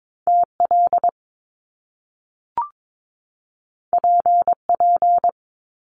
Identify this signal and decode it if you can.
Morse code, heard as a keyed tone, about 22 wpm, 700 Hz. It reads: TL PP
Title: Top Two Letter Combinations in Words 22wpm